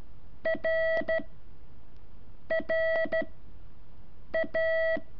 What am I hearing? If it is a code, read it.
Morse code, heard as a keyed tone, about 13 wpm, spaced as for 6 wpm, 650 Hz. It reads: R RA
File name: curso / ejercicio 4